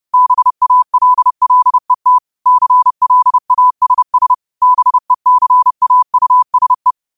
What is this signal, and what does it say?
Morse code, heard as a keyed tone, about 30 wpm, 1000 Hz. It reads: BALLET CLASS BECAUSE